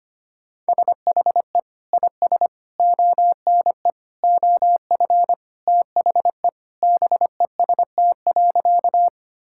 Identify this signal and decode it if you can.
Morse code, heard as a keyed tone, about 25 wpm, 700 Hz. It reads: SHE IS ONE OF THE BEST.
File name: practice / 25wpm / Sentences with 2-Word Phrases - 25wpm